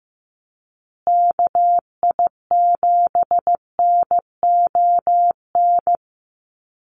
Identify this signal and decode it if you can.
Morse code, heard as a keyed tone, about 15 wpm, 700 Hz. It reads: KI7NON
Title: USA Call Signs Repeated 3x 15wpm